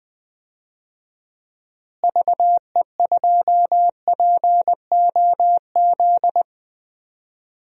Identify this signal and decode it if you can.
Morse code, heard as a keyed tone, about 20 wpm, 700 Hz. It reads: VE2POZ
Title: DXCC-100 Call Signs - 6 Characters - Repeated 2x - 20wpm